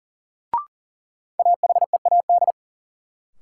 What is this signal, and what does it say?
Morse code, heard as a keyed tone, about 40 wpm, 700 Hz. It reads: AHEAD
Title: Sentences from Top 1100 Words - New Word Drill Alt - 40wpm